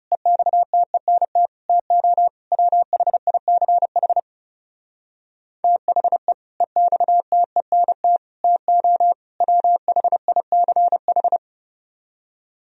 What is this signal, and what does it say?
Morse code, heard as a keyed tone, about 37 wpm, 700 Hz. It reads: EXTENT TO WHICH THE EXTENT TO WHICH
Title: Speed-Racing - Sets of 4 Words 30wpm